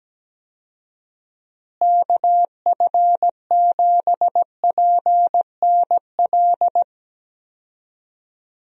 Morse code, heard as a keyed tone, about 17 wpm, 700 Hz. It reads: KF7PNL
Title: USA Call Signs - 6 Characters - Repeated 2x - 17wpm